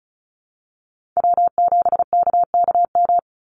Morse code, heard as a keyed tone, about 34 wpm, 700 Hz. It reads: W7KKM
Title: NA SOTA Chaser Callsigns 35wpm